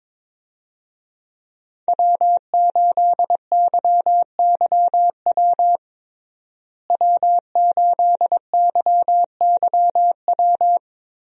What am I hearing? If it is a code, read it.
Morse code, heard as a keyed tone, about 22 wpm, 690 Hz. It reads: W8YYW W8YYW
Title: USA Call Signs - 5 Characters - Repeated 3x - 22wpm